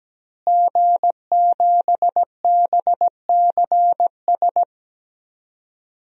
Morse code, heard as a keyed tone, about 17 wpm, 700 Hz. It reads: G7BCS
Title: DXCC-100 Call Signs - 5 Characters - Repeated 3x - 17wpm